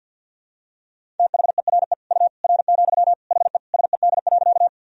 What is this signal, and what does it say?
Morse code, heard as a keyed tone, about 51 wpm, 700 Hz. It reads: THERE U R, HE SED.
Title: Sentences from Top 100 Words and Abbreviations 50wpm